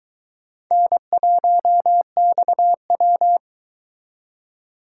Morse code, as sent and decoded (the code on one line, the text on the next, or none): -. .---- -..- .--
N1XW